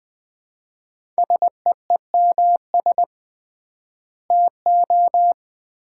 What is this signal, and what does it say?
Morse code, heard as a keyed tone, about 21 wpm, 700 Hz. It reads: SEEMS TO
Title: 3x Word-Spacing - Sets of 3 Words - 20wpm